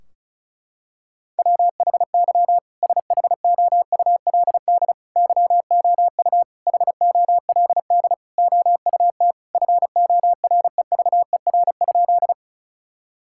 Morse code, heard as a keyed tone, about 34 wpm, 700 Hz. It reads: WHY SHOULD YOU HOLD OUT FOREVER?
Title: Sentences from Top 1600 Words 35wpm